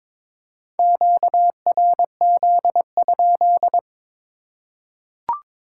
Morse code, heard as a keyed tone, about 22 wpm, 690 Hz. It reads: QRZ?
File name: Top 90 QSO Elements - 4x Character Spacing - 22wpm